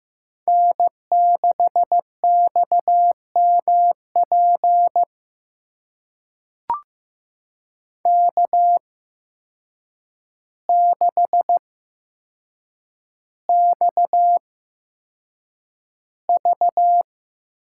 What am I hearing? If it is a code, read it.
Morse code, heard as a keyed tone, about 15 wpm, spaced as for 7 wpm, 700 Hz. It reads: N6XMP K6XV
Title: USA Call Signs - 8x Character Spacing - 15wpm